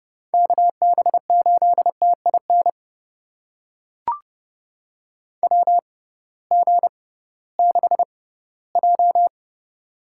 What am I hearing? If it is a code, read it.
Morse code, heard as a keyed tone, about 30 wpm, spaced as for 18 wpm, 700 Hz. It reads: KB8TIN WG6J
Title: USA Call Signs - 6x Character Spacing - 30wpm